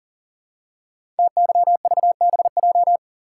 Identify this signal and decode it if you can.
Morse code, heard as a keyed tone, about 40 wpm, 700 Hz. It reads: TYVBJ
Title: Code Groups - 5 Letters - 40wpm